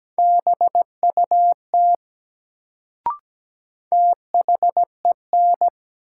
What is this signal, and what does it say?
Morse code, heard as a keyed tone, about 17 wpm, 700 Hz. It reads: BUT THEN
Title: Sentences from Top 100 Words - New Word Drill - 17wpm